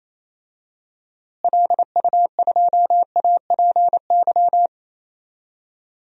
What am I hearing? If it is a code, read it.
Morse code, heard as a keyed tone, about 28 wpm, 700 Hz. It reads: LU2APY
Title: DXCC-100 Call Signs - 6 Characters - Repeated 2x - 28wpm